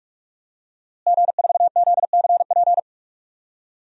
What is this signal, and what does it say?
Morse code, heard as a keyed tone, about 46 wpm, 700 Hz. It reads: G4ZCP
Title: DXCC-100 Call Signs - 5 Characters - Repeated 2x - 45wpm